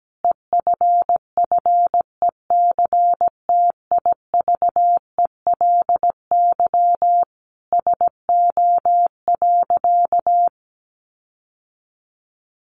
Morse code, heard as a keyed tone, about 17 wpm, 700 Hz. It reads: EFFECTIVELY SO.
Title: Sentences from Top 1400 Words 17wpm